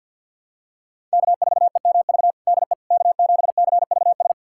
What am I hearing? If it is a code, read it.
Morse code, heard as a keyed tone, about 52 wpm, 700 Hz. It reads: K4EMV DE K7CVS
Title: USA Callsign DE Callsign - 5 Characters - 50wpm